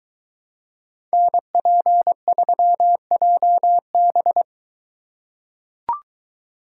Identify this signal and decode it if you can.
Morse code, heard as a keyed tone, about 21 wpm, 700 Hz. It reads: NP3JB